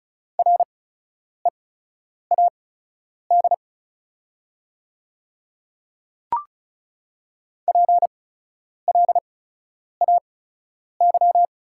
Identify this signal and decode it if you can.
Morse code, heard as a keyed tone, about 34 wpm, spaced as for 10 wpm, 690 Hz. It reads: READ PLAY
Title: 8x Character Spacing - Top 200 Words - 35wpm